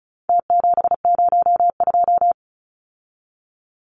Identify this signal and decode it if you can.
Morse code, heard as a keyed tone, about 34 wpm, 700 Hz. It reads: T702
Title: Numbers - Contest Style - 4-Digit 35wpm